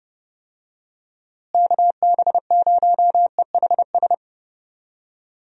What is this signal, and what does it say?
Morse code, heard as a keyed tone, about 30 wpm, 700 Hz. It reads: KB0EHS
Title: USA Call Signs - 6 Characters - Repeated 2x - 30wpm